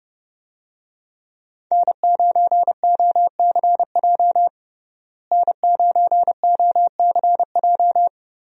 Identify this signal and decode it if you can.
Morse code, heard as a keyed tone, about 30 wpm, 700 Hz. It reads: N9OCJ N9OCJ
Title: USA Call Signs - 5 Characters - Repeated 3x - 30wpm